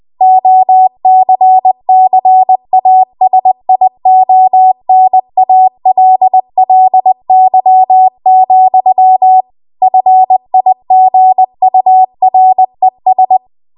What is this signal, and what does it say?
Morse code, heard as a keyed tone, about 20 wpm, 750 Hz. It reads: OCCASIONALLY, FIGURES
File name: Archive / 20 WPM / 20WPM